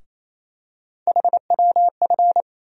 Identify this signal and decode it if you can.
Morse code, heard as a keyed tone, about 27 wpm, 700 Hz. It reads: HWF
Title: USA Call Sign Postfixes - 3 Letters 28wpm